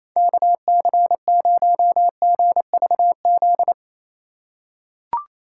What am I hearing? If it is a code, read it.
Morse code, heard as a keyed tone, about 28 wpm, 700 Hz. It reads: KC0GVZ
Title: USA Call Signs - 2x Character Spacing - 28wpm